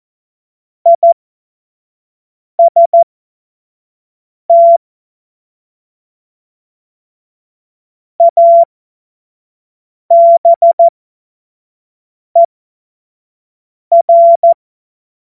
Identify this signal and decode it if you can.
Morse code, heard as a keyed tone, about 14 wpm, spaced as for 5 wpm, 680 Hz. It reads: IST ABER
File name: Morsekurs / Witze / Witz 19. Oktober 2021, 25 F70